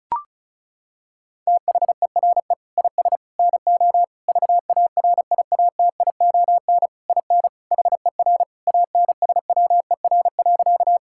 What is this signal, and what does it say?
Morse code, heard as a keyed tone, about 34 wpm, 700 Hz. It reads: THERE IS NO VARIATION IN HER ANSWER.
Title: Sentences from Top 1600 Words 35wpm